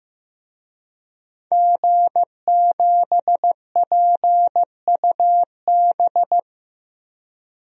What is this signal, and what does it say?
Morse code, heard as a keyed tone, about 15 wpm, 700 Hz. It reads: G7PUB
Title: DXCC-100 Call Signs - 5 Characters - Repeated 3x - 15wpm